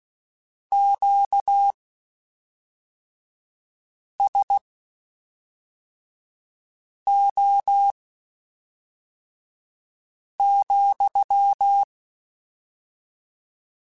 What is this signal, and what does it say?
Morse code, heard as a keyed tone, about 16 wpm, spaced as for 3 wpm, 800 Hz. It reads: QSO,
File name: (spaced) / QSOs Spaced-mp3s / kc7sac @5 sp